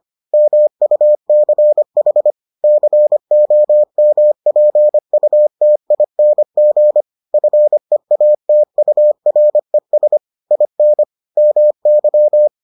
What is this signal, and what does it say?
Morse code, heard as a keyed tone, about 25 wpm, 600 Hz. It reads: MUCH COMPUTING FEATURES IN MY